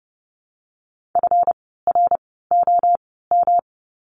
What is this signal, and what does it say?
Morse code, heard as a keyed tone, about 30 wpm, spaced as for 17 wpm, 700 Hz. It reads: FROM
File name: course / 30:10wpm / 237 Lesson 38f - Letter q - Words 30wpm